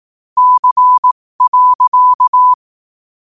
Morse code, heard as a keyed tone, about 18 wpm, spaced as for 15 wpm, 1000 Hz. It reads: C.
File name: qotdfiles / 15wpm